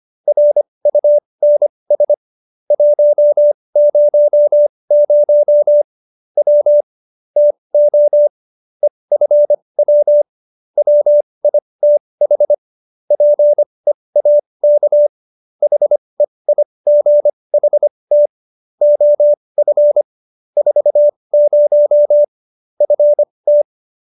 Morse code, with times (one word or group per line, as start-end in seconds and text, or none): 0.3-2.1 RUNS
2.7-5.8 100
6.4-6.8 W
7.4-8.3 TO
8.8-10.2 EFW
10.8-12.5 WITH
13.1-15.1 PEAK
15.6-18.3 HEIGHT
18.8-20.0 OF
20.6-22.2 40
22.8-23.6 FT